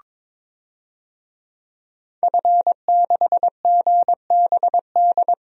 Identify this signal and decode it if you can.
Morse code, heard as a keyed tone, about 22 wpm, 700 Hz. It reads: F6GBD
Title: DXCC-100 Call Signs - 5 Characters - 22wpm